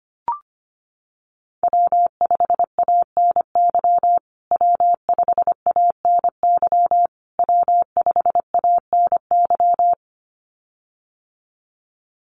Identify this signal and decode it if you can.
Morse code, heard as a keyed tone, about 25 wpm, 700 Hz. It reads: W5ANY W5ANY W5ANY
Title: USA Call Signs Repeated 3x 25wpm